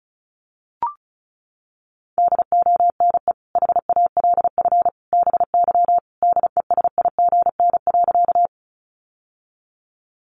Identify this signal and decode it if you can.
Morse code, heard as a keyed tone, about 34 wpm, 700 Hz. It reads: DONE HALF BY DESIGN.
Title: Sentences from Top 800 Words 35wpm